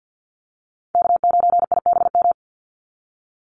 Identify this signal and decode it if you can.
Morse code, heard as a keyed tone, about 50 wpm, 700 Hz. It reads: K9IBM
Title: Call Signs Repeated 2x - 50wpm